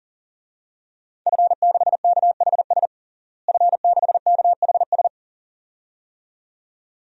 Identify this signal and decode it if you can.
Morse code, heard as a keyed tone, about 40 wpm, 700 Hz. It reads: F6KHS F6KHS
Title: DXCC-100 Call Signs - 5 Characters - Repeated 2x - 40wpm